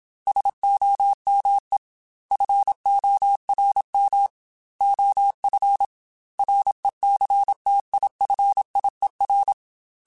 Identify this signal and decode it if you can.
Morse code, heard as a keyed tone, about 27 wpm, 800 Hz. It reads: SOME FORM OF RECTIFIER
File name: Technical Subjects / Technical Subjects-mp3s / Sw Pwr Sup @25